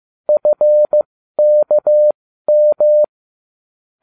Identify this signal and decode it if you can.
Morse code, heard as a keyed tone, about 15 wpm, spaced as for 12 wpm, 600 Hz. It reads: FKM